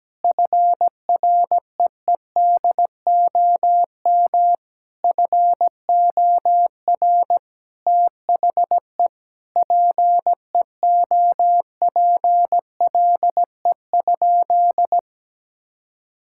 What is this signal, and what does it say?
Morse code, heard as a keyed tone, about 17 wpm, 700 Hz. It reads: FREEDOM FOR THE PEOPLE?
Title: Sentences from Top 1000 Words - Review Part 5 - 17wpm